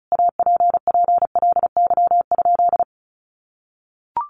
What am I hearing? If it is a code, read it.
Morse code, heard as a keyed tone, about 34 wpm, 700 Hz. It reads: APPLY?
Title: Sentences from Top 1200 Words 35wpm